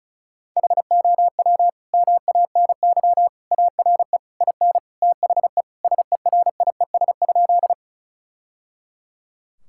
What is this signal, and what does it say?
Morse code, heard as a keyed tone, about 36 wpm, 690 Hz. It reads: HOW MANY ARE IN THE SERIES?
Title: Sentences from Top 1000 Words - Review Part 5 - 35wpm